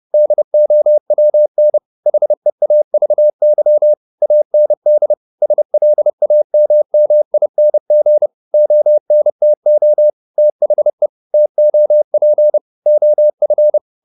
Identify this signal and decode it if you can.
Morse code, heard as a keyed tone, about 30 wpm, 600 Hz. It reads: DOWN HEAVY AND SLAMMING ONTO THE TOP OF